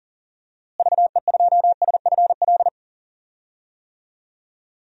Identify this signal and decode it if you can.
Morse code, heard as a keyed tone, about 40 wpm, 700 Hz. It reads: VE2SFL
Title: DXCC-100 Call Signs - 6 Characters - Repeated 3x - 40wpm